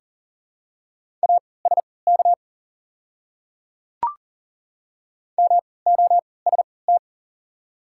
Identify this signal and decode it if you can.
Morse code, heard as a keyed tone, about 40 wpm, spaced as for 23 wpm, 690 Hz. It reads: ASK MOST
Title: 3x Character Spacing - Top 200 Words - 40wpm